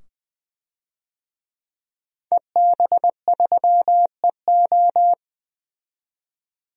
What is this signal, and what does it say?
Morse code, heard as a keyed tone, about 20 wpm, 700 Hz. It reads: EB3EO
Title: DXCC-100 Call Signs - 5 Characters - 20wpm